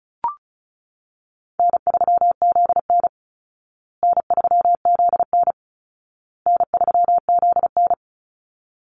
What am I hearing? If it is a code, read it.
Morse code, heard as a keyed tone, about 34 wpm, 700 Hz. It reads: N3ZN N3ZN N3ZN
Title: Call Signs Repeated 2x - 35wpm